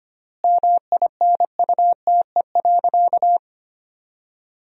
Morse code, heard as a keyed tone, about 25 wpm, 700 Hz. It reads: MINUTE.
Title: 4x Word-Spacing - Sentences from Top 500 Words - 25wpm